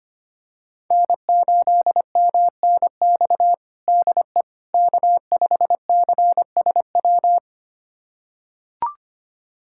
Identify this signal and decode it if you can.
Morse code, heard as a keyed tone, about 25 wpm, 700 Hz. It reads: N8MNX DE K5CSW
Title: USA Callsign DE Callsign 25wpm